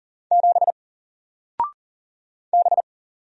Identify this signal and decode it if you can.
Morse code, heard as a keyed tone, about 40 wpm, 700 Hz. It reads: 7 B
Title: Single Character (Letter:Number:Punctuation:Prosign) Alt 40wpm